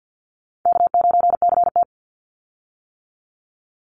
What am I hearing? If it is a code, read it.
Morse code, heard as a keyed tone, about 50 wpm, 700 Hz. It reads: K9CT
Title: Call Signs - 50wpm